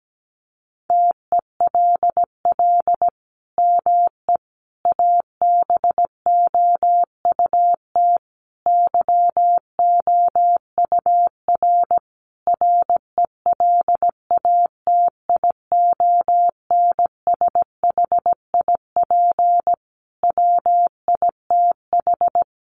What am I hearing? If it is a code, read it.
Morse code, heard as a keyed tone, about 17 wpm, 700 Hz. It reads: TELL ME ABOUT YOUR RELATIONSHIP WITH
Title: Sentences from Top 1100 Words 17wpm